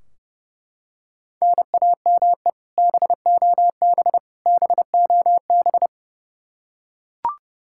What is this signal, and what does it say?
Morse code, heard as a keyed tone, about 30 wpm, 700 Hz. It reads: NAME BOB BOB
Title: K1USN SST QSO Practice 30wpm